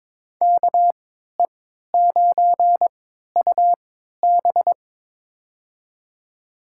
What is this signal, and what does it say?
Morse code, heard as a keyed tone, about 22 wpm, spaced as for 13 wpm, 700 Hz. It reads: KE9UB
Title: USA Call Signs - 5 Characters - 3x Character Spacing - 22wpm